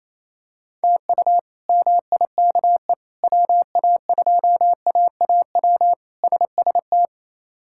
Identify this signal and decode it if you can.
Morse code, heard as a keyed tone, about 28 wpm, 700 Hz. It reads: TU MIKE WA2AAW SST
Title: K1USN SST QSO Practice - 28wpm